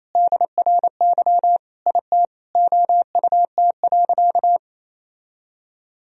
Morse code, heard as a keyed tone, about 28 wpm, 700 Hz. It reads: DRY IT OUT.
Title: Sentences from Top 1000 Words - Review Part 3 - 28wpm